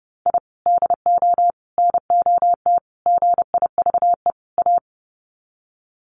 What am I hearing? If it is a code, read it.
Morse code, heard as a keyed tone, about 30 wpm, 700 Hz. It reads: I DO NOT GIVE A
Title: Sets of 5 Words Encore2 30wpm